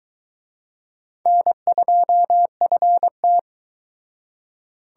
Morse code, heard as a keyed tone, about 23 wpm, 700 Hz. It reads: N2FT